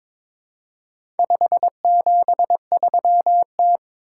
Morse code, heard as a keyed tone, about 22 wpm, 700 Hz. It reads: 573T
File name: practice / 22wpm / Four Letter-Number Combinations - 22wpm